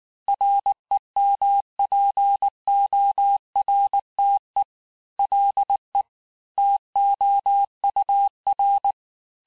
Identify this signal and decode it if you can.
Morse code, heard as a keyed tone, about 19 wpm, 800 Hz. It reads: REMPORTE LE TOUR
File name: cours-du-lundi files / cours-1589-texte-clair-vitesse-18mpm-puis-23mpm